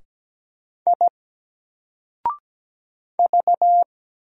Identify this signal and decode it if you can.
Morse code, heard as a keyed tone, about 17 wpm, 700 Hz. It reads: I V
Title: Single Letter-Number 17wpm